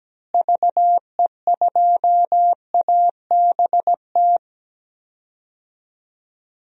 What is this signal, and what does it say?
Morse code, heard as a keyed tone, about 17 wpm, 700 Hz. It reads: VE2ABT
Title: DXCC-100 Call Signs - 6 Characters - Repeated 3x - 17wpm